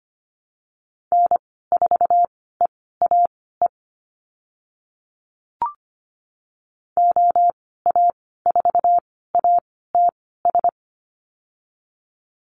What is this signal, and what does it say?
Morse code, heard as a keyed tone, about 25 wpm, spaced as for 16 wpm, 700 Hz. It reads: N4EAE OA4ATS
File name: Lesson 15g - Number 5 - Call Signs 25wpm